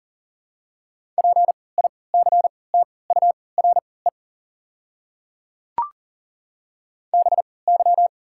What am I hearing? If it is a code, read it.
Morse code, heard as a keyed tone, about 40 wpm, spaced as for 23 wpm, 700 Hz. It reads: PICTURE BY
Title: 3x Character Spacing - Top 200 Words - 40wpm